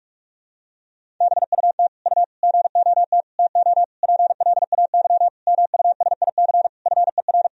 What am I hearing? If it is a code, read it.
Morse code, heard as a keyed tone, about 41 wpm, 700 Hz. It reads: BUT U GOT TO PLAY MUSIC FER
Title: Sentences from Top 300 Words and Abbreviations 45wpm